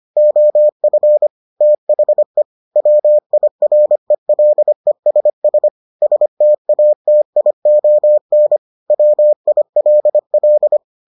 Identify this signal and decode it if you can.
Morse code, heard as a keyed tone, about 25 wpm, 600 Hz. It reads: OF THE WIRELESS STATION WILL